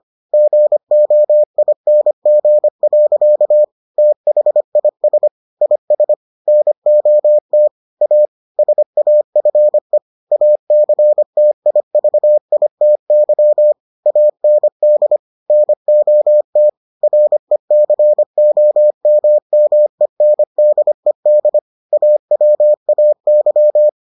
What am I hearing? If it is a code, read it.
Morse code, heard as a keyed tone, about 25 wpm, 600 Hz. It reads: GOING. THIS IS NOT A SAFE ACTIVITY AND NOT RECOMMENDED AWAY